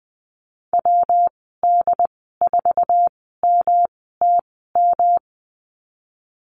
Morse code, heard as a keyed tone, about 20 wpm, spaced as for 15 wpm, 700 Hz. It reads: WD4MTM
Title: 122 Lesson 20g - Number 3 - Call Signs 20wpm